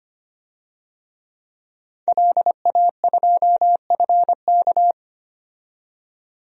Morse code, heard as a keyed tone, about 25 wpm, 700 Hz. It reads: LA2FK
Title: DXCC-100 Call Signs - 5 Characters - 25wpm